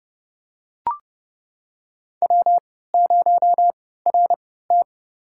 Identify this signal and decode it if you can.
Morse code, heard as a keyed tone, about 30 wpm, spaced as for 17 wpm, 700 Hz. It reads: W0RT
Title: USA Call Signs - 4 Characters - 3x Character Spacing - 30wpm